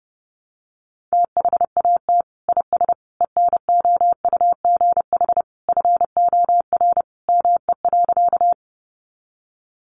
Morse code, heard as a keyed tone, about 30 wpm, 700 Hz. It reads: THAT IS ENOUGH FOR ME.
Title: Sentences from Top 400 Words 30wpm